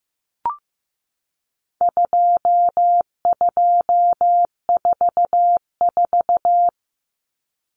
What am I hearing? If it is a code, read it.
Morse code, heard as a keyed tone, about 15 wpm, 700 Hz. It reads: 2244
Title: Common Cut Numbers - 4-Digit 15wpm